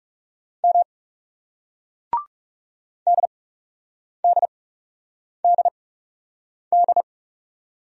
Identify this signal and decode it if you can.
Morse code, heard as a keyed tone, about 39 wpm, spaced as for 8 wpm, 700 Hz. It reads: M DDDD